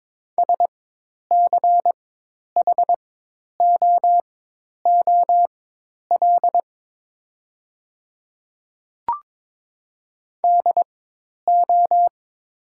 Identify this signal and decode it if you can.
Morse code, heard as a keyed tone, about 22 wpm, spaced as for 10 wpm, 700 Hz. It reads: SCHOOL DO
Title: 4x Character Spacing - Top 300 Words - 22wpm